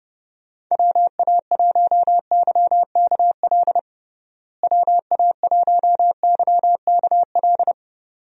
Morse code, heard as a keyed tone, about 30 wpm, 700 Hz. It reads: WA1YKL WA1YKL
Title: USA Call Signs - 6 Characters - Repeated 2x - 30wpm